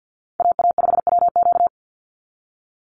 Morse code, heard as a keyed tone, about 50 wpm, 700 Hz. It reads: AA5WQ
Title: USA Call Signs 50wpm